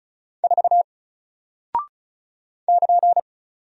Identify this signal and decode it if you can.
Morse code, heard as a keyed tone, about 36 wpm, 700 Hz. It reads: 4 <KN>
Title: Single Character (Letter:Number:Punctuation:Prosign) Alt 35wpm